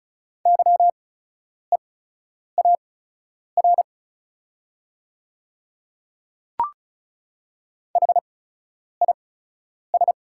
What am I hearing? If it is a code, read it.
Morse code, heard as a keyed tone, about 36 wpm, spaced as for 10 wpm, 690 Hz. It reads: YEAR HIS
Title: 8x Character Spacing - Top 300 Words - 35wpm